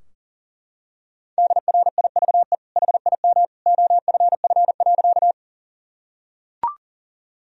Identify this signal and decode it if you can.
Morse code, heard as a keyed tone, about 40 wpm, 700 Hz. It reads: DRIVE HIM OFF.
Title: Sentences from Top 1000 Words - Review Part 2 - 40wpm